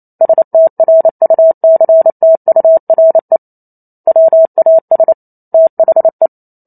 Morse code, heard as a keyed tone, about 28 wpm, 650 Hz. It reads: STRUCTURE WAS THE